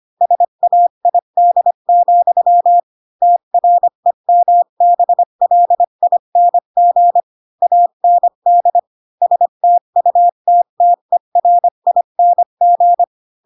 Morse code, heard as a keyed tone, about 25 wpm, 700 Hz. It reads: SAID, TREMBLING AND STUTTERING
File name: Pinocchio / Pinocchio ch1 20 25 ik1hge